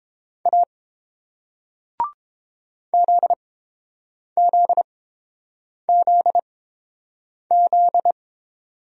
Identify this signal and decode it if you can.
Morse code, heard as a keyed tone, about 30 wpm, spaced as for 7 wpm, 700 Hz. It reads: A ZZZZ